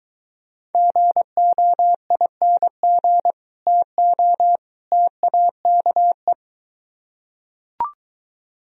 Morse code, heard as a keyed tone, about 23 wpm, 700 Hz. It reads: GOING TO TAKE